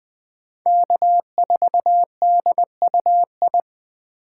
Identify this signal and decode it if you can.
Morse code, heard as a keyed tone, about 24 wpm, 700 Hz. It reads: K4DUI
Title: Speed Racing - USA Call Signs 20wpm